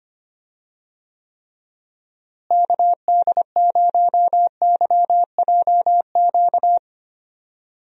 Morse code, heard as a keyed tone, about 25 wpm, 700 Hz. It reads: KD0YJQ
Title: USA Call Signs - 6 Characters - Repeated 3x - 25wpm